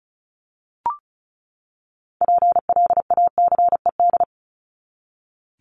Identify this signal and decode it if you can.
Morse code, heard as a keyed tone, about 34 wpm, 700 Hz. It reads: PLACED
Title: ED Suffix - Top 100 Words with Base Words 35wpm